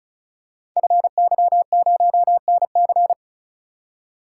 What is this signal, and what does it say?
Morse code, heard as a keyed tone, about 35 wpm, 700 Hz. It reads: FY0NC
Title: Code Groups - 5 Letter:Numbers - 35wpm